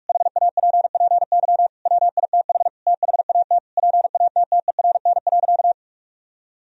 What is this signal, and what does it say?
Morse code, heard as a keyed tone, about 46 wpm, 700 Hz. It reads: HAPPY WITH THAT PATTERN.
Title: Sentences from Top 1000 Words - Review Part 2 - 45wpm